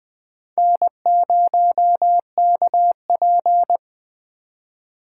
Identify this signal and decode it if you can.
Morse code, heard as a keyed tone, about 20 wpm, 700 Hz. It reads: N0KP
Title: Speed Racing - USA 1x2 Call Signs 17wpm